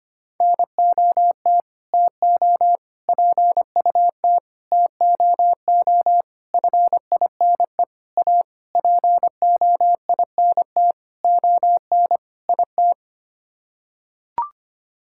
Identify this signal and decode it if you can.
Morse code, heard as a keyed tone, about 25 wpm, 700 Hz. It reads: NOT TO PUT TOO FINE A POINT ON IT
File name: Sets of 9 Words - 25wpm